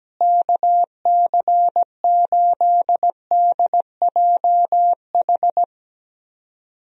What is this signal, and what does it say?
Morse code, heard as a keyed tone, about 17 wpm, 700 Hz. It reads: KC8DJH
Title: USA Call Signs - 6 Characters - Repeated 3x - 17wpm